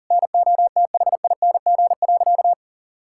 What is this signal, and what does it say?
Morse code, heard as a keyed tone, about 37 wpm, 700 Hz. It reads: NOTHING.
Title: Sentences from Top 1000 Words - Review Part 6 - 40wpm